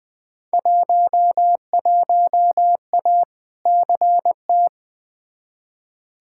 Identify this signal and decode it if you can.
Morse code, heard as a keyed tone, about 20 wpm, 700 Hz. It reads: 11A CT
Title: Field Day Exchanges Encore - 20wpm